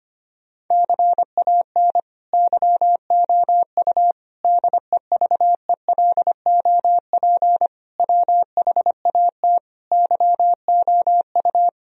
Morse code, heard as a keyed tone, about 25 wpm, 700 Hz. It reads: CAN YOU DEVELOP WHAT YOU
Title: Sentences from Top 1000 Words - Review Part 2 - 25wpm